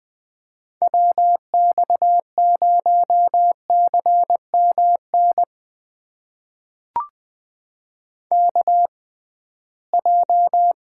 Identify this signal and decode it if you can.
Morse code, heard as a keyed tone, about 20 wpm, spaced as for 15 wpm, 700 Hz. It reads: WX0CMN KJ
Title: USA Call Signs - 6 Characters - 6x Character Spacing - 20wpm